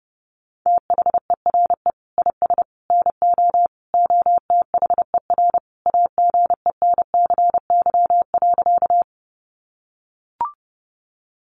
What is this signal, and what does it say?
Morse code, heard as a keyed tone, about 30 wpm, 700 Hz. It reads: THERE IS NO OTHER AGENCY.
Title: Sentences from Top 1300 Words 30wpm